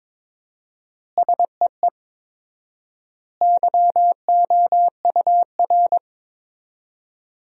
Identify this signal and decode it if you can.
Morse code, heard as a keyed tone, about 22 wpm, 700 Hz. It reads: SEE YOUR
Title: 4x Word-Spacing - Sentences from Top 500 Words - 22wpm